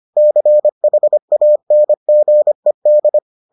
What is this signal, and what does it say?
Morse code, heard as a keyed tone, about 25 wpm, 600 Hz. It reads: CHANGED